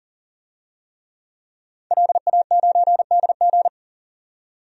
Morse code, heard as a keyed tone, about 37 wpm, 700 Hz. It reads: LA9DG
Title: DXCC-100 Call Signs - 5 Characters - 40wpm